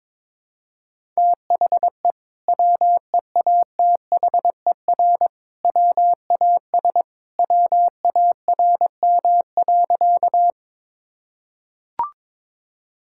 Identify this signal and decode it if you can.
Morse code, heard as a keyed tone, about 22 wpm, 700 Hz. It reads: THE WEATHER WAS WARM.
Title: Sentences from Top 1000 Words - Review Part 5 - 22wpm